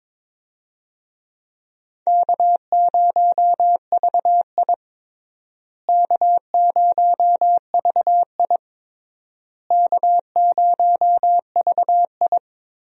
Morse code, heard as a keyed tone, about 22 wpm, 700 Hz. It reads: K0VI K0VI K0VI
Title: USA Call Signs - 4 Characters - Repeated 3x - 22wpm